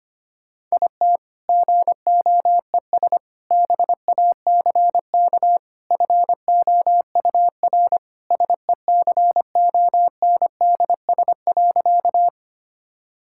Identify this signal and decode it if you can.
Morse code, heard as a keyed tone, about 25 wpm, 700 Hz. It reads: IT GOES BACK FOUR SECONDS.
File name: practice / 25wpm / All-Age Appropriate Jokes - 25wpm